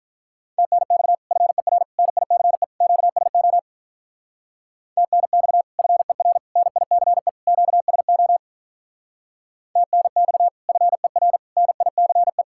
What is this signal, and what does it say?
Morse code, heard as a keyed tone, about 46 wpm, 700 Hz. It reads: TNX FER NICE QSO TNX FER NICE QSO TNX FER NICE